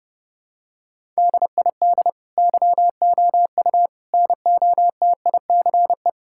Morse code, heard as a keyed tone, about 32 wpm, 700 Hz. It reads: DID YOU NOTICE
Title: Sentences from Top 1000 Words - Review Part 5 - 30wpm